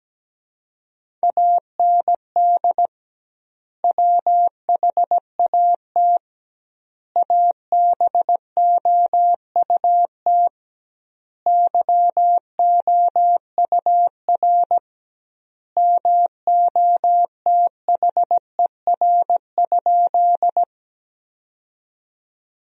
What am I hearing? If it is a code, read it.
Morse code, heard as a keyed tone, about 17 wpm, 700 Hz. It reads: AND WHAT ABOUT YOUR MOTHER?
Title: 2x Word-Spacing - Sentences from Top 200 Words - 17wpm